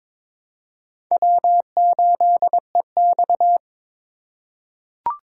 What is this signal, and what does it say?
Morse code, heard as a keyed tone, about 22 wpm, 700 Hz. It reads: W8EX
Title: USA Call Signs - 4 Characters - 3x Character Spacing - 22wpm